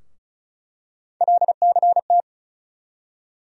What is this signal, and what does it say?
Morse code, heard as a keyed tone, about 36 wpm, 700 Hz. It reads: LCT
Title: USA Call Sign Postfixes - 3 Letters 35wpm